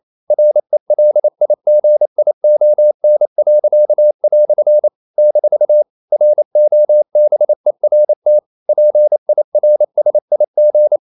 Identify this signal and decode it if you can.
Morse code, heard as a keyed tone, about 28 wpm, 600 Hz. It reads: RELIGION." - ROBERT PIRSIG
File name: CW / Practice / EnglishQuotations0128